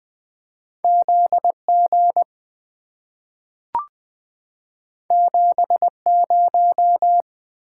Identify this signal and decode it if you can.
Morse code, heard as a keyed tone, about 20 wpm, 700 Hz. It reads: ZG 70
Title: Two Letter-Number Combinations - 20wpm